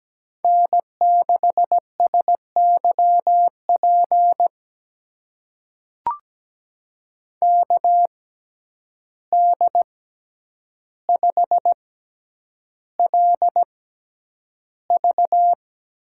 Morse code, heard as a keyed tone, about 17 wpm, spaced as for 9 wpm, 700 Hz. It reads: N6SYP KD5LV
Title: USA Call Signs - 5 Characters - 6x Character Spacing - 17wpm